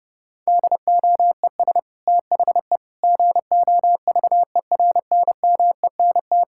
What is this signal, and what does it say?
Morse code, heard as a keyed tone, about 30 wpm, 700 Hz. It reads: DOES THE GOVERNMENT
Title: Sentences from Top 1600 Words 30wpm